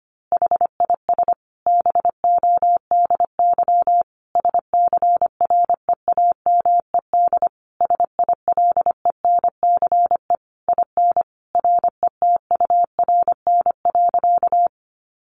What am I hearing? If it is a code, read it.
Morse code, heard as a keyed tone, about 25 wpm, 700 Hz. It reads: HIS BODY SCREAMED SILENCE IN RETURN.
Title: A Sound Of Thunder 25wpm